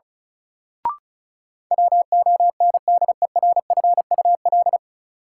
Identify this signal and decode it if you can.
Morse code, heard as a keyed tone, about 36 wpm, 700 Hz. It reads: WONDERFUL
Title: Sentences from Top 1100 Words - New Word Drill Alt - 35wpm